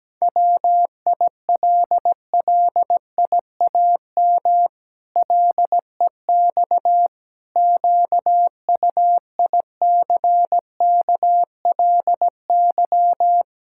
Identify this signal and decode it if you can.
Morse code, heard as a keyed tone, about 17 wpm, 700 Hz. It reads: WILLIAM LEX QUICKLY
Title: Pangram Challenge v2 - 17wpm